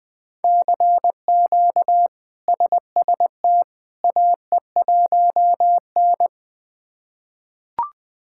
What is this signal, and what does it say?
Morse code, heard as a keyed tone, about 20 wpm, 700 Hz. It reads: CQ SST AE1N
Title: K1USN SST QSO Practice - 20wpm